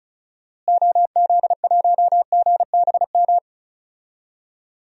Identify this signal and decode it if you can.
Morse code, heard as a keyed tone, about 36 wpm, 700 Hz. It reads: OZ1GBM
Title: DXCC-100 Call Signs - 6 Characters - Repeated 3x - 35wpm